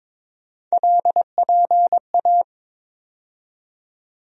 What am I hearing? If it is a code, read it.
Morse code, heard as a keyed tone, about 22 wpm, 700 Hz. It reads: LPA
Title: USA Call Sign Postfixes - 3 Letters 22wpm